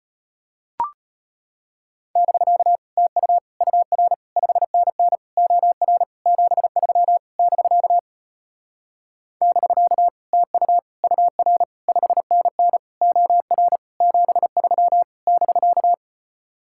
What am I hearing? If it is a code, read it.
Morse code, heard as a keyed tone, about 33 wpm, 700 Hz. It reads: <BK> TU UR 5NN OR 73 <BK> <BK> TU UR 5NN OR 73 <BK>